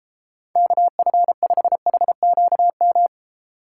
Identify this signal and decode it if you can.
Morse code, heard as a keyed tone, about 32 wpm, 700 Hz. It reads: KF5HQM